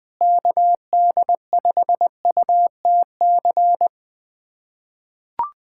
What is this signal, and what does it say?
Morse code, heard as a keyed tone, about 21 wpm, 700 Hz. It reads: KD5UTC
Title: USA Callsign DE Callsign 20wpm